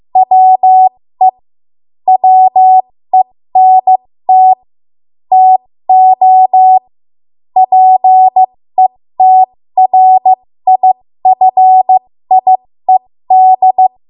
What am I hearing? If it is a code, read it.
Morse code, heard as a keyed tone, about 15 wpm, spaced as for 13 wpm, 750 Hz. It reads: WE WENT TO PETRIFIED
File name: Archive / 13 WPM / 13WPM